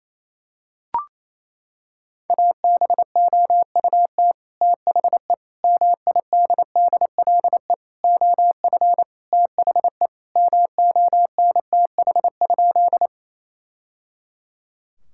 Morse code, heard as a keyed tone, about 28 wpm, 700 Hz. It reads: ABOUT THE MIDDLE OF THE MONTH?
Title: Sentences from Top 1000 Words - Review Part 6 - 28wpm